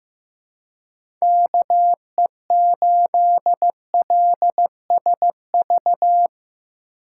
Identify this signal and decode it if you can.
Morse code, heard as a keyed tone, about 15 wpm, 700 Hz. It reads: KE8LSV
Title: USA Call Signs - 2x Character Spacing - 15wpm